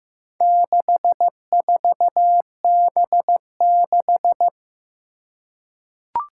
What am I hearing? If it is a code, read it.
Morse code, heard as a keyed tone, about 15 wpm, 700 Hz. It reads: 64B6
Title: Four Challenging Letter-Number Combinations 15wpm